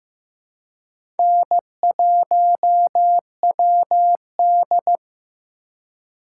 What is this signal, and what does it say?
Morse code, heard as a keyed tone, about 15 wpm, 700 Hz. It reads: N1WD
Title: USA Call Signs - 4x Character Spacing - 15wpm